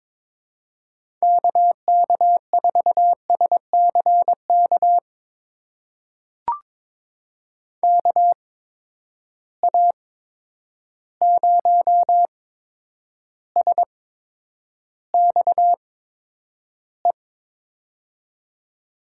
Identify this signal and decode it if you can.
Morse code, heard as a keyed tone, about 22 wpm, spaced as for 9 wpm, 700 Hz. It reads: KK4SCK KA0SXE
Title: USA Call Signs - 6 Characters - 8x Character Spacing - 22wpm